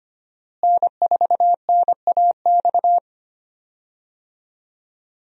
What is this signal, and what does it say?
Morse code, heard as a keyed tone, about 25 wpm, 700 Hz. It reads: N4NAX
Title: USA Call Signs - 5 Characters - Repeated 3x - 25wpm